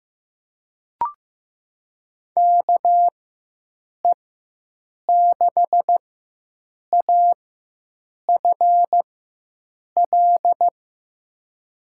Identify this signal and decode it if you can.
Morse code, heard as a keyed tone, about 15 wpm, spaced as for 7 wpm, 700 Hz. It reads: KE6AFL
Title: USA Call Signs - 6 Characters - 4x Character Spacing - 15wpm